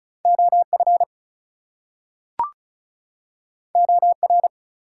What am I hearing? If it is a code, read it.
Morse code, heard as a keyed tone, about 36 wpm, 700 Hz. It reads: OF OR
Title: Two Letter Words - 35wpm